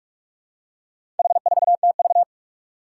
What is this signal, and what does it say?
Morse code, heard as a keyed tone, about 42 wpm, 700 Hz. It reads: H4TV
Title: Four Challenging Letter-Number Combinations 45wpm